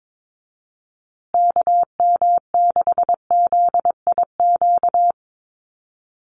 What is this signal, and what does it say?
Morse code, heard as a keyed tone, about 22 wpm, 700 Hz. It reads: KM6ZIQ
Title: USA Call Signs Repeated 2x 22wpm